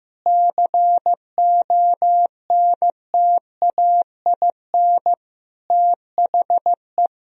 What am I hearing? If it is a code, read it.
Morse code, heard as a keyed tone, about 15 wpm, 700 Hz. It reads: CONTAIN THE
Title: Sentences from Top 1000 Words - Review Part 2 - 15wpm